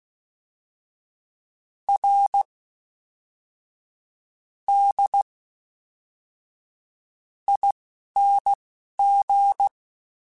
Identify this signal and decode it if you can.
Morse code, heard as a keyed tone, about 16 wpm, spaced as for 12 wpm, 800 Hz. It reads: R D ING
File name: Special INGs / Special INGs-mp3s / Fast ing @10